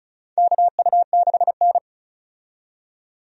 Morse code, heard as a keyed tone, about 36 wpm, 700 Hz. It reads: KU6N
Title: USA Call Signs - 4 Characters - 35wpm